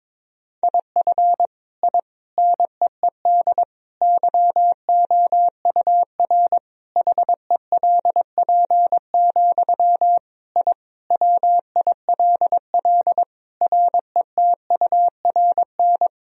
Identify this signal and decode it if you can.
Morse code, heard as a keyed tone, about 21 wpm, 700 Hz. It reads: IF I NEED YOUR HELP, I WILL RETURN
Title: Sentences from Top 1000 Words - Review Part 1 - 22wpm